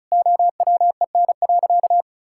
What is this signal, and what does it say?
Morse code, heard as a keyed tone, about 34 wpm, 700 Hz. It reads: OWEN.
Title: Pangram Challenge v2 - 35wpm